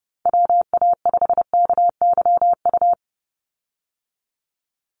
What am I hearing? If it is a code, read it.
Morse code, heard as a keyed tone, about 30 wpm, 700 Hz. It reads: WA5KYU
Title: USA Call Signs 30wpm